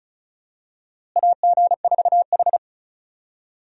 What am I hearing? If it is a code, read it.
Morse code, heard as a keyed tone, about 34 wpm, 700 Hz. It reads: AG4H
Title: USA Call Signs - 4 Characters - Repeated 2x - 35wpm